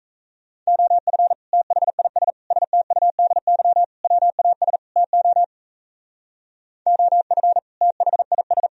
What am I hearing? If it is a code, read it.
Morse code, heard as a keyed tone, about 33 wpm, 700 Hz. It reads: OF THIS STUDY WAS TO OF THIS
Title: Speed-Racing - Sets of 5 Words 28wpm